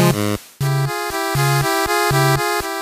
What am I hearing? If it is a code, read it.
There is no code.